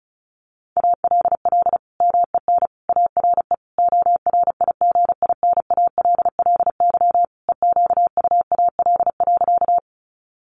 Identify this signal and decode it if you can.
Morse code, heard as a keyed tone, about 34 wpm, 700 Hz. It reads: ALL MEN ARE ORIGINALLY EQUAL.
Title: Sentences from Top 1500 Words 35wpm